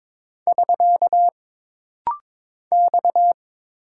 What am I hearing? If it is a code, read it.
Morse code, heard as a keyed tone, about 22 wpm, 700 Hz. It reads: <SK> X
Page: This practice set is composed of Letters, Numbers, common punctuations, and common prosigns.